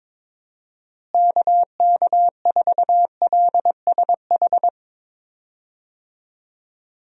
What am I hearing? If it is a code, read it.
Morse code, heard as a keyed tone, about 22 wpm, 700 Hz. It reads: KK4LSH
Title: USA Call Signs - 6 Characters - Repeated 3x - 22wpm